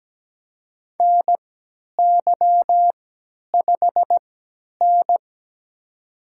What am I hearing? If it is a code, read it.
Morse code, heard as a keyed tone, about 17 wpm, spaced as for 10 wpm, 700 Hz. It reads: NY5N
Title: USA Call Signs - 3x Character Spacing - 17wpm